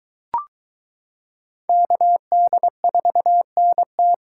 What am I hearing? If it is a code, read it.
Morse code, heard as a keyed tone, about 23 wpm, 700 Hz. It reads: KD4NT